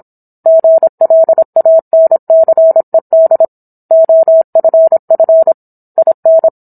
Morse code, heard as a keyed tone, about 27 wpm, 650 Hz. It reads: GLANCED OFF IN